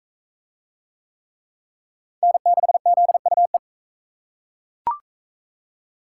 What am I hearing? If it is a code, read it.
Morse code, heard as a keyed tone, about 38 wpm, 700 Hz. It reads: N6ZUE